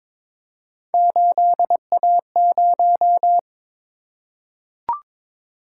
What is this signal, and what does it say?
Morse code, heard as a keyed tone, about 22 wpm, 700 Hz. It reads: 8A0
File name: practice / 22wpm / Three Letter-Number Combinations - 22wpm